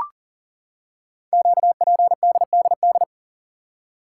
Morse code, heard as a keyed tone, about 40 wpm, 700 Hz. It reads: QPDDD
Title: Code Groups - 5 Letters - 40wpm